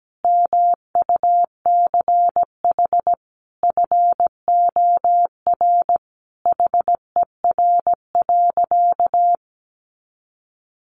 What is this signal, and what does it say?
Morse code, heard as a keyed tone, about 17 wpm, 700 Hz. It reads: MUCH FOR HER.
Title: Sentences from Top 1200 Words 17wpm